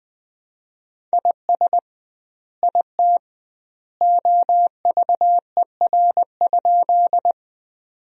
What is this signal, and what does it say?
Morse code, heard as a keyed tone, about 20 wpm, 700 Hz. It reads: IS IT OVER?
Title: 2x Word-Spacing - Sentences from Top 100 Words - 20wpm